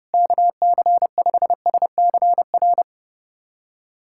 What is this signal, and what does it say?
Morse code, heard as a keyed tone, about 30 wpm, 700 Hz. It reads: KC5SCR
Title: USA Callsign DE Callsign - 6 Characters - 30wpm